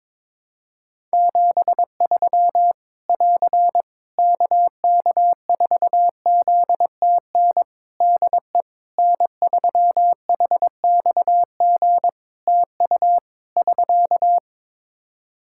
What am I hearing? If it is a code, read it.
Morse code, heard as a keyed tone, about 22 wpm, 700 Hz. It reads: 73 <AR> KK4ZTN DE N3HXG TU <SK>
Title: Practice QSO - Exchange 4 - Ending - 22wpm